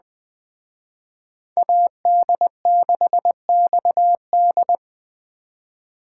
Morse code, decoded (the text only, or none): AD6XD